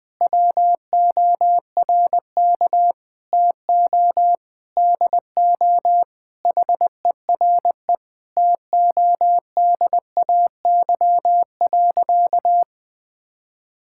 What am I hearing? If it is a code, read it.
Morse code, heard as a keyed tone, about 20 wpm, 700 Hz. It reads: WORK TO DO HERE TODAY.
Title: Sentences with 7-Word Phrases - Spoken Beginning - 20wpm